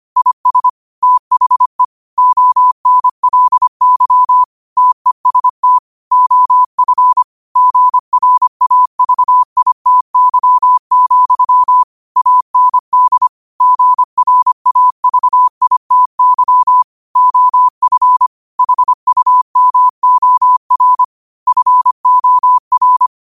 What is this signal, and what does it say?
Morse code, heard as a keyed tone, about 25 wpm, 1000 Hz. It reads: IS THE ONLY TEST OF GRAVITY, AND GRAVITY OF HUMOR FOR
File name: qotdfiles / 25wpm